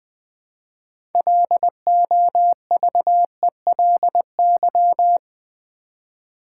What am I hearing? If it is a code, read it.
Morse code, heard as a keyed tone, about 20 wpm, 700 Hz. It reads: LOVELY
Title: 6 Letter Words - First 3 Letters Given - 20wpm